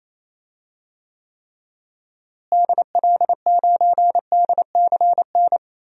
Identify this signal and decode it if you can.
Morse code, heard as a keyed tone, about 27 wpm, 700 Hz. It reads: DL9DCN
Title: DXCC-100 Call Signs - 6 Characters - Repeated 3x - 28wpm